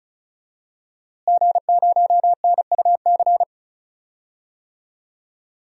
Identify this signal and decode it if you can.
Morse code, heard as a keyed tone, about 36 wpm, 700 Hz. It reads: G0NUC
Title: DXCC-100 Call Signs - 5 Characters - Repeated 2x - 35wpm